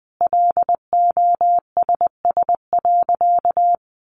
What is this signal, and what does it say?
Morse code, heard as a keyed tone, about 20 wpm, 700 Hz. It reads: LOSS.
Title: Sentences from Top 1400 Words 20wpm